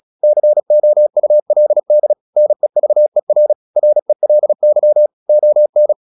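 Morse code, heard as a keyed tone, about 36 wpm, 600 Hz. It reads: COULD NEVER RELY ON
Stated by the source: Morse practice at 36 wpm, text of Ken Follett's novel Never, from CW med Gnister.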